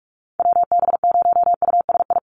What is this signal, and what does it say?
Morse code, heard as a keyed tone, about 46 wpm, 700 Hz. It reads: WB0USI
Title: NA SOTA Activator Callsigns 45wpm